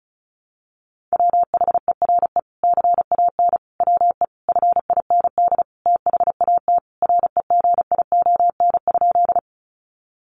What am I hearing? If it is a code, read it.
Morse code, heard as a keyed tone, about 34 wpm, 700 Hz. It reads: WHERE CAN WE FIND THAT REGION?
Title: Sentences from Top 1500 Words 35wpm